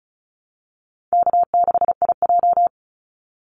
Morse code, heard as a keyed tone, about 34 wpm, 700 Hz. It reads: K6IJ
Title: Easy Call Signs Repeated 2x - 35wpm